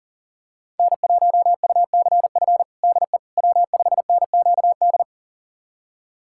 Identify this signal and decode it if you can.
Morse code, heard as a keyed tone, about 40 wpm, 700 Hz. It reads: N1UCF DE W5NQD
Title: USA Callsign DE Callsign - 5 Characters - 40wpm